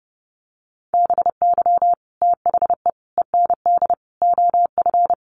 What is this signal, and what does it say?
Morse code, heard as a keyed tone, about 30 wpm, 700 Hz. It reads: BY THE END OF
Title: Sets of 4 Words 30wpm